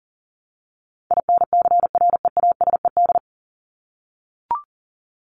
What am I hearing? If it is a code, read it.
Morse code, heard as a keyed tone, about 40 wpm, 700 Hz. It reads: INCREASED
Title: ED Suffix - Top 100 Words with Base Words 40wpm